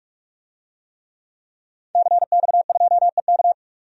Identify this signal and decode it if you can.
Morse code, heard as a keyed tone, about 45 wpm, 700 Hz. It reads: CX2EK
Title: DXCC-100 Call Signs - 5 Characters - 45wpm